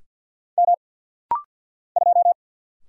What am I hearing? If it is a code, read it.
Morse code, heard as a keyed tone, about 50 wpm, 700 Hz. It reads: M 2
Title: Single Letter-Number Alt - Rapid-Fire 50wpm